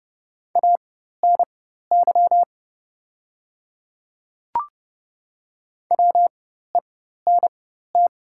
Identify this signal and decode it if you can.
Morse code, heard as a keyed tone, about 30 wpm, spaced as for 14 wpm, 700 Hz. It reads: ANY WENT
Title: 4x Character Spacing - Top 300 Words - 30wpm